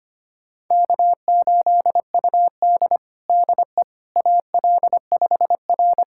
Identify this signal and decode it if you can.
Morse code, heard as a keyed tone, about 25 wpm, 700 Hz. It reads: K8UD DE AL5R
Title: USA Callsign DE Callsign - 4 Characters - 25wpm